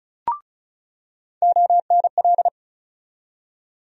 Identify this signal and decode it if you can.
Morse code, heard as a keyed tone, about 34 wpm, 700 Hz. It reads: ONL
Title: USA Call Sign Postfixes - 3 Letters 35wpm